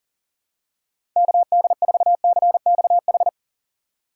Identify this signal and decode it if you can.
Morse code, heard as a keyed tone, about 37 wpm, 700 Hz. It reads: KD4CXH
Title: USA Call Signs - 6 Characters - 2x Character Spacing - 40wpm